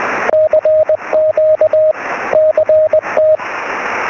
Morse code, heard as a keyed tone, about 20 wpm, 620 Hz, with radio noise. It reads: CQ CT
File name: sm7lcb / 2208 ly2baw 23cm